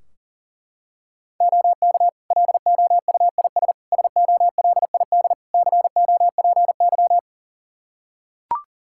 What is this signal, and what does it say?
Morse code, heard as a keyed tone, about 40 wpm, 700 Hz. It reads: OK LOUIS SOLID COPY
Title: Practice QSO - Exchange 3 - More Info - 40wpm